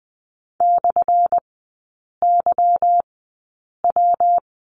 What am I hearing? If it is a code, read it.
Morse code, heard as a keyed tone, about 20 wpm, spaced as for 8 wpm, 700 Hz. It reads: /YW
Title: 198 Lesson 32e - Letter k - Three Characters 20wpm